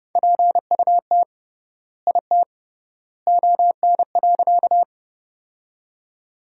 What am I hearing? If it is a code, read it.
Morse code, heard as a keyed tone, about 30 wpm, 700 Hz. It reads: PUT IT ON.
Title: 3x Word-Spacing - Sentences from Top 200 Words - 30wpm